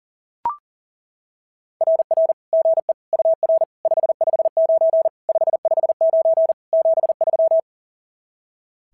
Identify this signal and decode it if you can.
Morse code, heard as a keyed tone, about 40 wpm, 660 Hz. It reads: RR GE UR 559 559 73
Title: SOTA QSO Practice 40wpm